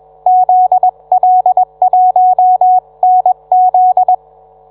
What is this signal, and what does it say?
Morse code, heard as a keyed tone, about 21 wpm, 740 Hz. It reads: ZL1NZ